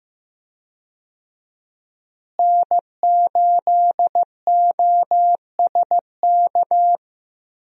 Morse code, heard as a keyed tone, about 16 wpm, 700 Hz. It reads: N8OSK
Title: USA Call Signs - 5 Characters - Repeated 2x - 15wpm